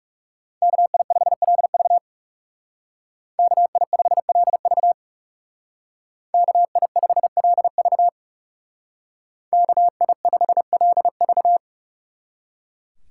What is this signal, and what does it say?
Morse code, heard as a keyed tone, about 36 wpm, 700 Hz. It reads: KI5LV KI5LV KI5LV KI5LV
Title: Speed Racing - USA 2x2 Call Signs 30wpm